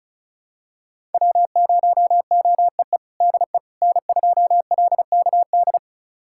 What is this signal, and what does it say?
Morse code, heard as a keyed tone, about 36 wpm, 700 Hz. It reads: W0OEE DE N2LKD
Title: USA Callsign DE Callsign - 5 Characters - 35wpm